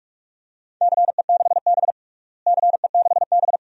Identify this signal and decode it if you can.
Morse code, heard as a keyed tone, about 46 wpm, 700 Hz. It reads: CE6B CE6B
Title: DXCC-100 Call Signs - 4 Characters - Repeated 2x - 45wpm